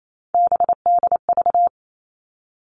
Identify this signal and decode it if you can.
Morse code, heard as a keyed tone, about 28 wpm, 700 Hz. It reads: BDV
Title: All Cut Numbers - 3-Digit 28wpm